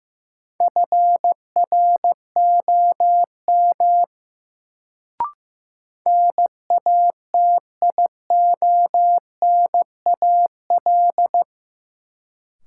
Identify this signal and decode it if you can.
Morse code, heard as a keyed tone, about 15 wpm, 700 Hz. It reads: FROM NATIONAL
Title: Sets of 2 Words - New Word Drill - 15wpm